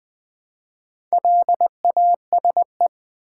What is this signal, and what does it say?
Morse code, heard as a keyed tone, about 20 wpm, 700 Hz. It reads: LASE
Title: Build A Word - 1 Letter At A Time - 5-Letter Words - 20wpm